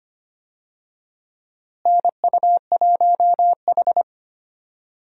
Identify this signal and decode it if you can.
Morse code, heard as a keyed tone, about 25 wpm, 700 Hz. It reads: NU1H
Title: USA Call Signs - 4 Characters - Repeated 3x - 25wpm